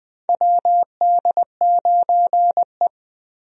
Morse code, heard as a keyed tone, about 20 wpm, 700 Hz. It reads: WD9E
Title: USA Call Signs - 4 Characters - 20wpm